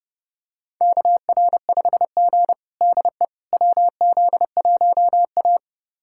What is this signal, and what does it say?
Morse code, heard as a keyed tone, about 32 wpm, 700 Hz. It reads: KR5G DE WZ1A
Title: USA Callsign DE Callsign - 4 Characters - 30wpm